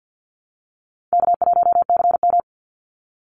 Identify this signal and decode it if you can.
Morse code, heard as a keyed tone, about 48 wpm, 700 Hz. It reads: K1CM
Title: NA SOTA Chaser Callsigns 50wpm